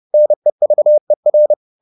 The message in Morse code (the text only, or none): NEVER